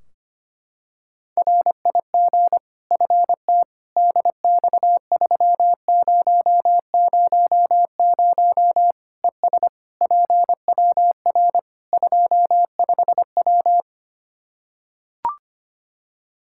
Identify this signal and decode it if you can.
Morse code, heard as a keyed tone, about 25 wpm, 700 Hz. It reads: RIG FT DX3000 ES PWR 25W
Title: K1USN SST QSO Practice 25wpm